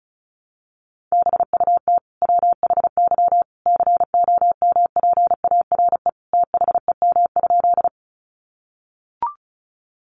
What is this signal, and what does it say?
Morse code, heard as a keyed tone, about 34 wpm, 700 Hz. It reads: BUT WHY COMPARE THEM?
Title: Sentences from Top 1300 Words 35wpm